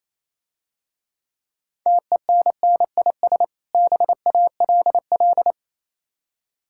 Morse code, heard as a keyed tone, about 28 wpm, 690 Hz. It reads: TENNIS BALL